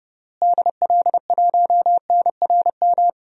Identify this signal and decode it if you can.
Morse code, heard as a keyed tone, about 30 wpm, 700 Hz. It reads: DL1NRM
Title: DXCC-100 Call Signs - 6 Characters - Repeated 2x - 30wpm